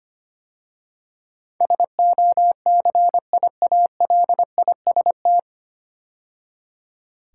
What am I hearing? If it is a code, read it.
Morse code, heard as a keyed tone, about 25 wpm, 700 Hz. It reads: SOCIALIST
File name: practice / 25wpm / 9 Letter Words - First 3 Letters Given - 25wpm